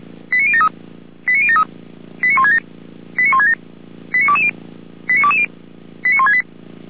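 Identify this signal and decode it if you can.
Five-tone selective call (ZVEI), heard as quick runs of five tones, tones of about 80 ms. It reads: 89073 89073 89167 89167 89220 89220 89168
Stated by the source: Five-tone ZVEI selective call received on an SDR, from audio.f85360.feuerwehr-orlamuende.org (Feuerwehr Orlamünde fire brigade).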